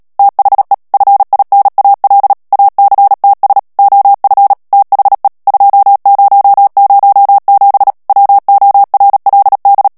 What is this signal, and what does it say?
Morse code, heard as a keyed tone, about 37 wpm, 800 Hz. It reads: THE FINAL ACTS OF THE 2007 WORLD
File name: morse mp3 / ANGLAIS texte clair 3 vitesse 35 mots minute